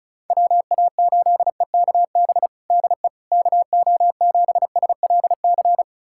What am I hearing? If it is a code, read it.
Morse code, heard as a keyed tone, about 36 wpm, 700 Hz. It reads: WA8EKB DE KO7SLC
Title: USA Callsign DE Callsign - 6 Characters - 35wpm